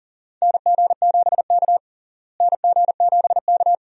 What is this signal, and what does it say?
Morse code, heard as a keyed tone, about 40 wpm, 700 Hz. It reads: NG7K NG7K
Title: USA Call Signs - 4 Characters - Repeated 2x - 40wpm